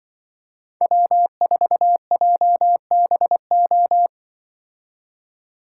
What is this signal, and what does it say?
Morse code, heard as a keyed tone, about 24 wpm, 700 Hz. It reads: W4JBO
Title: Speed Racing - USA 1x3 Call Signs 20wpm